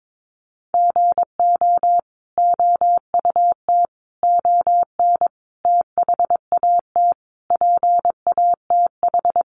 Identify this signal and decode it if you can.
Morse code, heard as a keyed tone, about 22 wpm, 700 Hz. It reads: GO OUT ON THAT PATH
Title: A Sound Of Thunder 22wpm